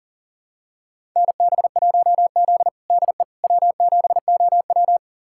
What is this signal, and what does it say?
Morse code, heard as a keyed tone, about 40 wpm, 700 Hz. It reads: NB1Z DE W7OW
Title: USA Callsign DE Callsign - 4 Characters - 40wpm